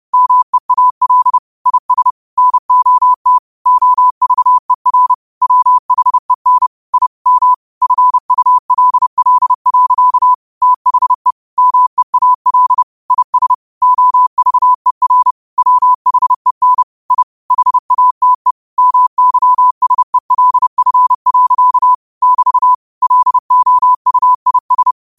Morse code, heard as a keyed tone, about 30 wpm, 1000 Hz. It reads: MEAL IS NOT OVER WHEN I M FULL. THE MEAL IS OVER WHEN I HATE MYSELF. <BT> LOUIS
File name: qotdfiles / 30wpm